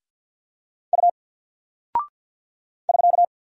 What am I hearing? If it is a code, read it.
Morse code, heard as a keyed tone, about 48 wpm, 700 Hz. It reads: U <SK>
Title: Single Character (Letter:Number:Punctuation:Prosign) Alt 50wpm